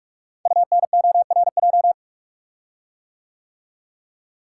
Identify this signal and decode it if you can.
Morse code, heard as a keyed tone, about 45 wpm, 690 Hz. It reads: UNORJ
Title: Code Groups - 5 Letters - 45wpm